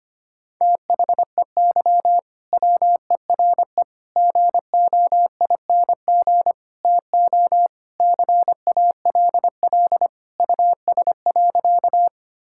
Morse code, heard as a keyed tone, about 25 wpm, 700 Hz. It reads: THEY WERE GOING TO CALL US.
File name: practice / 25wpm / Sentences with 4-Word Phrases - 25wpm